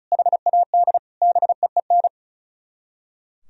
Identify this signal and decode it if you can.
Morse code, heard as a keyed tone, about 45 wpm, 700 Hz. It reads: HAD BEEN